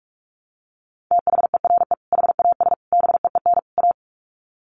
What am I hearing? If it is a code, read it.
Morse code, heard as a keyed tone, about 44 wpm, 700 Hz. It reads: THERE HAS BEEN A
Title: Sets of 4 Words 45wpm